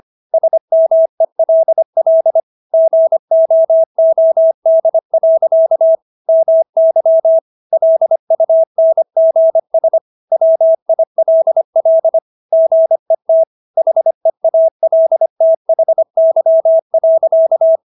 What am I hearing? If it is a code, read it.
Morse code, heard as a keyed tone, about 25 wpm, 650 Hz. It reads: SMELL GOOD. MY LUNGS WILL GET HEALTHY.